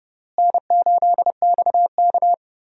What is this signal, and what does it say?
Morse code, heard as a keyed tone, about 32 wpm, 700 Hz. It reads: N8XK
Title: USA Call Signs - 4 Characters - Repeated 3x - 30wpm